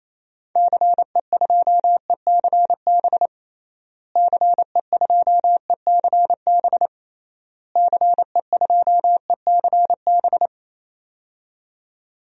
Morse code, heard as a keyed tone, about 27 wpm, 700 Hz. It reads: CE2ECB CE2ECB CE2ECB
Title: DXCC-100 Call Signs - 6 Characters - Repeated 3x - 28wpm